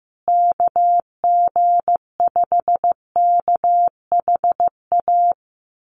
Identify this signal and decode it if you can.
Morse code, heard as a keyed tone, about 15 wpm, 700 Hz. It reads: KG5KHA
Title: USA Call Signs Repeated 3x 15wpm